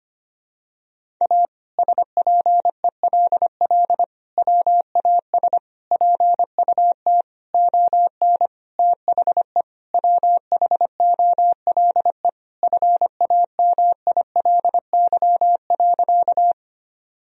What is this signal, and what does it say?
Morse code, heard as a keyed tone, about 25 wpm, 700 Hz. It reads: A SPELL WAS PUT ON THE WHOLE FAMILY.
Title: Sentences from Top 1000 Words - Review Part 1 - 25wpm